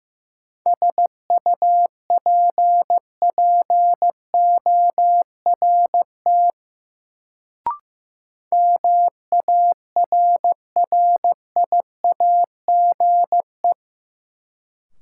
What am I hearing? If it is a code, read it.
Morse code, heard as a keyed tone, about 15 wpm, 700 Hz. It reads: SUPPORT MARRIAGE
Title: Sentences from Top 800 Words - New Word Drill - 15wpm